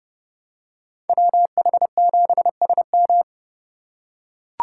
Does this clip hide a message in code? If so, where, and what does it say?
Yes: Morse code from 1.1 to 3.2 s, heard as a keyed tone, about 29 wpm, 700 Hz. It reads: WH7SM